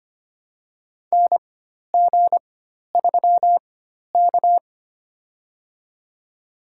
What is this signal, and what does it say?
Morse code, heard as a keyed tone, about 25 wpm, spaced as for 12 wpm, 700 Hz. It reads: NG3K
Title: USA Call Signs - 4 Characters - 4x Character Spacing - 25wpm